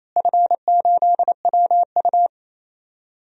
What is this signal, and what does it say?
Morse code, heard as a keyed tone, about 28 wpm, 700 Hz. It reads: F8WU
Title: DXCC-100 Call Signs - 4 Characters - Repeated 2x - 28wpm